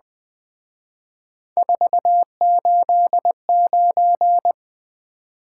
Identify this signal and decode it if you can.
Morse code, heard as a keyed tone, about 21 wpm, 700 Hz. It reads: 489